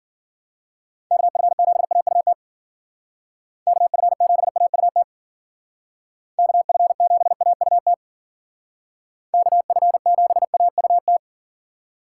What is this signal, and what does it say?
Morse code, heard as a keyed tone, about 49 wpm, 700 Hz. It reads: KF7AUT KF7AUT KF7AUT KF7AUT
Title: Speed Racing - USA 2x3 Call Signs 40wpm